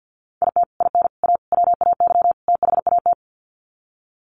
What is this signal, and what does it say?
Morse code, heard as a keyed tone, about 50 wpm, 700 Hz. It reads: IT IN A WAY THAT
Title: Sets of 5 Words Encore2 50wpm